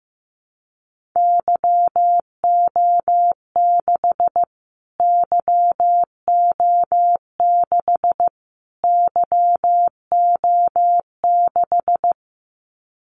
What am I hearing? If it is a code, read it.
Morse code, heard as a keyed tone, about 15 wpm, 700 Hz. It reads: YO6 YO6 YO6
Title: Callsign Prefixes for Top 25 DXCC - Repeated 3x - 15wpm